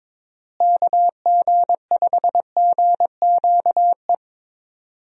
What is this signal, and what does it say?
Morse code, heard as a keyed tone, about 22 wpm, 700 Hz. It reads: KG5GQE
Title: USA Call Signs - 6 Characters - Repeated 3x - 22wpm